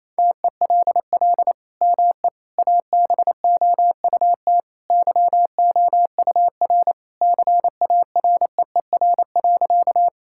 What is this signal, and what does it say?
Morse code, heard as a keyed tone, about 28 wpm, 700 Hz. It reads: TELL ME ABOUT YOUR CAREER.
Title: Sentences from Top 1000 Words - Review Part 6 - 28wpm